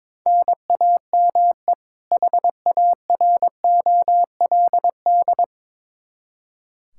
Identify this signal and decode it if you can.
Morse code, heard as a keyed tone, about 22 wpm, 700 Hz. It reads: NAME HAROLD
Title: Practice QSO - Exchange 1 - Beginning - 22wpm